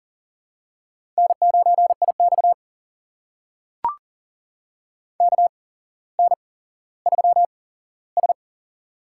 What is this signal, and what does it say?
Morse code, heard as a keyed tone, about 42 wpm, spaced as for 17 wpm, 700 Hz. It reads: N9IX KN3S
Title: USA Call Signs - 4 Characters - 8x Character Spacing - 40wpm